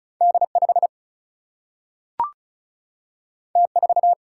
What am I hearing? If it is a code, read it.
Morse code, heard as a keyed tone, about 36 wpm, 700 Hz. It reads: D5 T4
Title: Single-Letter Single-Number 35wpm